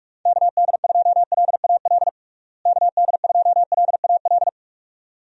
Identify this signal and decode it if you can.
Morse code, heard as a keyed tone, about 41 wpm, 700 Hz. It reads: KD2LAL KD2LAL
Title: USA Call Signs - 6 Characters - Repeated 2x - 45wpm